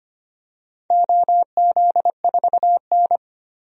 Morse code, heard as a keyed tone, about 25 wpm, 690 Hz. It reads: OZ4N
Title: DXCC-100 Call Signs - 4 Characters - 25wpm